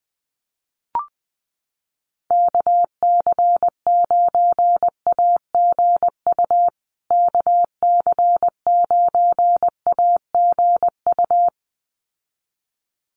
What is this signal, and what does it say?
Morse code, heard as a keyed tone, about 20 wpm, 700 Hz. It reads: KC9AGU KC9AGU
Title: USA Call Signs Repeated 2x 20wpm